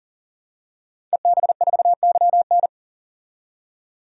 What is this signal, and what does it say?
Morse code, heard as a keyed tone, about 40 wpm, 700 Hz. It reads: EB4YN
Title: DXCC-100 Call Signs - 5 Characters - 40wpm